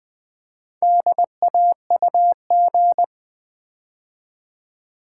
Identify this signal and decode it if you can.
Morse code, heard as a keyed tone, about 20 wpm, 700 Hz. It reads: DAUG
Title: Build A Word - 4 Letters At A Time - 8-Letter Words - 20wpm